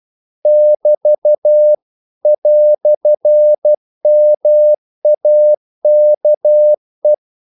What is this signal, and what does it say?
Morse code, heard as a keyed tone, about 12 wpm, 600 Hz. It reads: <BT> "MAKE
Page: English quotations in CW (telegraphy - morse)